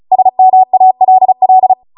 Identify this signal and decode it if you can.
Morse code, heard as a keyed tone, about 36 wpm, 750 Hz. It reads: SMALL